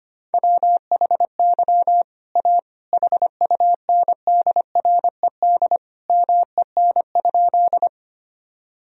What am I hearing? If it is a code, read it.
Morse code, heard as a keyed tone, about 25 wpm, 700 Hz. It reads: WHY A HUNDRED MEN?
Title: Sentences from Top 1000 Words - Review Part 2 - 25wpm